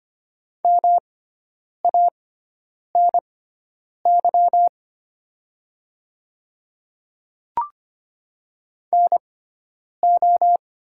Morse code, heard as a keyed tone, about 25 wpm, spaced as for 9 wpm, 700 Hz. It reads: MANY NO
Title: 6x Character Spacing - Top 200 Words - 25wpm